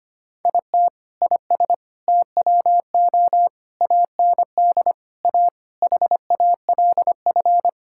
Morse code, heard as a keyed tone, about 25 wpm, 700 Hz. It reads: IT IS TWO AND A HALF
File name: practice / 25wpm / Sentences with 4-Word Phrases - 25wpm